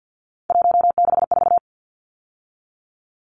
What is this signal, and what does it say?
Morse code, heard as a keyed tone, about 50 wpm, 700 Hz. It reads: 164
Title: Numbers - 3-Digit 50wpm